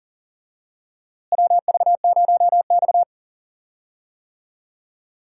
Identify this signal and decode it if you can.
Morse code, heard as a keyed tone, about 40 wpm, 700 Hz. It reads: WV0X
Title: USA Call Signs - 4 Characters - Repeated 2x - 40wpm